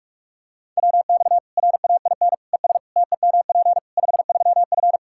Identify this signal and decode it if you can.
Morse code, heard as a keyed tone, about 43 wpm, 700 Hz. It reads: WX RAIN ES TEMP 53F
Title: Practice QSO - Exchange 2 - Info - 45wpm